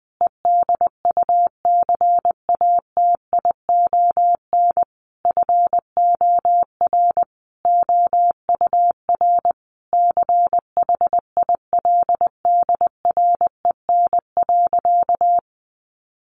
Morse code, heard as a keyed tone, about 20 wpm, 700 Hz. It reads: EDUCATION FOR OUR CHILDREN.
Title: Sentences from Top 1400 Words 20wpm